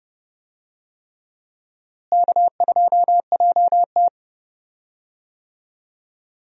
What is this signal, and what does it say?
Morse code, heard as a keyed tone, about 30 wpm, 700 Hz. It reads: K2JT
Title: USA Call Signs - 4 Characters - Repeated 3x - 30wpm